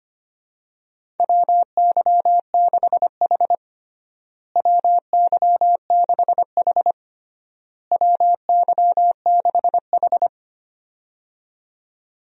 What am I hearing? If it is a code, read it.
Morse code, heard as a keyed tone, about 25 wpm, 700 Hz. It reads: WY6H WY6H WY6H
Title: USA Call Signs - 4 Characters - Repeated 3x - 25wpm